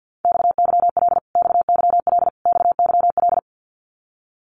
Morse code, heard as a keyed tone, about 50 wpm, 700 Hz. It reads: XYL XYL XYL
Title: Abbreviations Repeated 3x - 50wpm